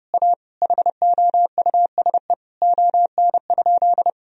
Morse code, heard as a keyed tone, about 30 wpm, 700 Hz. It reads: A HOUSE ON?
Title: Sentences from Top 1000 Words - Review Part 4 - 30wpm